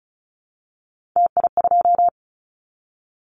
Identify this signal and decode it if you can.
Morse code, heard as a keyed tone, about 34 wpm, 700 Hz. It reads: TI2
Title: Callsign Prefixes for Top 100 DXCC - 35wpm